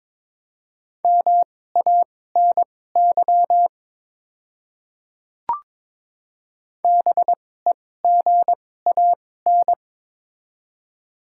Morse code, heard as a keyed tone, about 22 wpm, spaced as for 16 wpm, 700 Hz. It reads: MANY BEGAN
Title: 2x Character Spacing - Top 500 Words - 22wpm